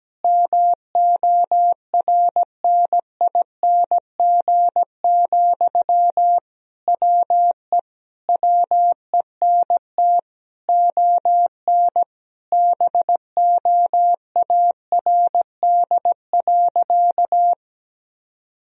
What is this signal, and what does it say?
Morse code, heard as a keyed tone, about 17 wpm, 700 Hz. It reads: MORNING, WE WENT ON BOARD.
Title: Sentences from Top 1000 Words - Review Part 4 - 17wpm